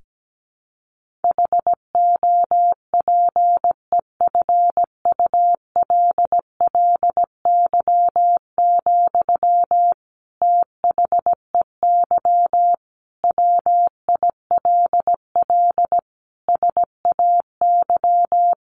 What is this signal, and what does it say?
Morse code, heard as a keyed tone, about 17 wpm, 700 Hz. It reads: HOPEFULLY, THEY WILL SAY
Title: Sentences from Top 1500 Words 17wpm